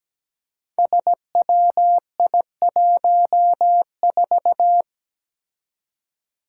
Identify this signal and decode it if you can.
Morse code, heard as a keyed tone, about 17 wpm, 700 Hz. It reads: SWI14
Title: Five Challenging Letter-Number Combinations 17wpm